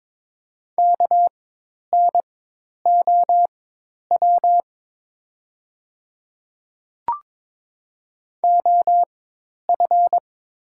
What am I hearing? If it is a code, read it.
Morse code, heard as a keyed tone, about 22 wpm, spaced as for 10 wpm, 690 Hz. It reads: KNOW OF
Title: 4x Character Spacing - Top 200 Words - 22wpm